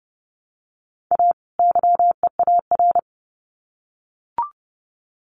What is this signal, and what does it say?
Morse code, heard as a keyed tone, about 30 wpm, 700 Hz. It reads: A YEAR
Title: Sets of 2 Words 30wpm